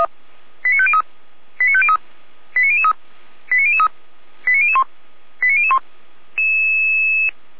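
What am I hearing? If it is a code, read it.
Five-tone selective call (ZVEI), heard as quick runs of five tones, tones of about 70 ms. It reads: 89503 89503 89003 89003 89001 89001
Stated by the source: ZVEI five-tone call received by SDR.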